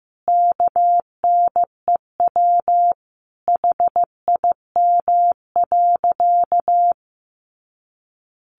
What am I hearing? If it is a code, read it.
Morse code, heard as a keyed tone, about 15 wpm, 700 Hz. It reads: KNEW HIM.
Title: Sentences from Top 1500 Words 15wpm